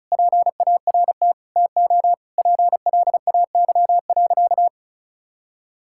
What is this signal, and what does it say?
Morse code, heard as a keyed tone, about 36 wpm, 700 Hz. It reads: PART TO PLAY.
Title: Sentences from Top 1000 Words - Review Part 6 - 35wpm